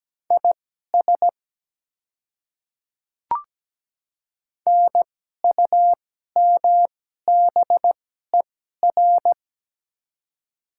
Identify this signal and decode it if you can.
Morse code, heard as a keyed tone, about 17 wpm, spaced as for 12 wpm, 700 Hz. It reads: IS NUMBER
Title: 2x Character Spacing - Top 100 Words - 17wpm